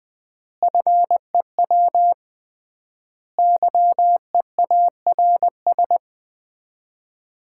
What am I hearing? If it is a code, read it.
Morse code, heard as a keyed tone, about 20 wpm, 700 Hz. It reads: FEW YEARS
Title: 3x Word-Spacing - Sets of 5 Words - 20wpm